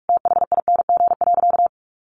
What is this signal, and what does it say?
Morse code, heard as a keyed tone, about 44 wpm, 700 Hz. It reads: THING.
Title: Sentences from Top 700 Words 45wpm